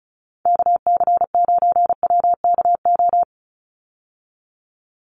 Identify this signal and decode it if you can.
Morse code, heard as a keyed tone, about 34 wpm, 700 Hz. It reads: KC9WKO
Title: USA Call Signs 35wpm